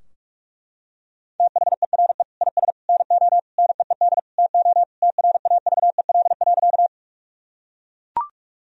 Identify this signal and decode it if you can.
Morse code, heard as a keyed tone, about 41 wpm, 700 Hz. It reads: THERE IS NO NEED TO TRAVEL.
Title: Sentences from Top 1000 Words - Review Part 2 - 45wpm